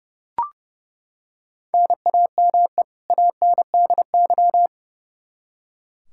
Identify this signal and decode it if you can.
Morse code, heard as a keyed tone, about 30 wpm, 700 Hz. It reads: NAME ANDY
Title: Practice QSO - Exchange 1 - Beginning - 30wpm